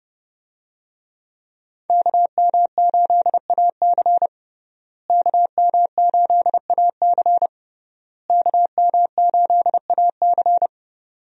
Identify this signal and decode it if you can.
Morse code, heard as a keyed tone, about 32 wpm, 700 Hz. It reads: KM8AC KM8AC KM8AC
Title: USA Call Signs - 5 Characters - Repeated 3x - 30wpm